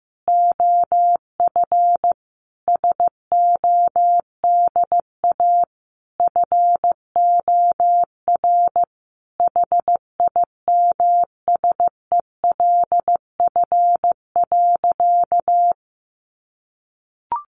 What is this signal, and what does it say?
Morse code, heard as a keyed tone, about 15 wpm, 700 Hz. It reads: OF SODA FOR HIMSELF.
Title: Additional Easy Phrases 15wpm